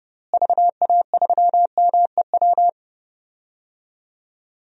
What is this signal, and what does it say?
Morse code, heard as a keyed tone, about 32 wpm, 700 Hz. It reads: VA3MEW
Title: DXCC-100 Call Signs - 6 Characters - Repeated 3x - 30wpm